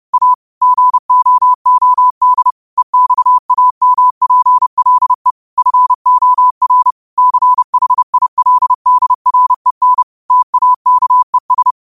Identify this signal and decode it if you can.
Morse code, heard as a keyed tone, about 30 wpm, 1000 Hz. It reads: A GOOD EXAMPLE FOR CHILDREN TAKES
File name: qotdfiles / 30wpm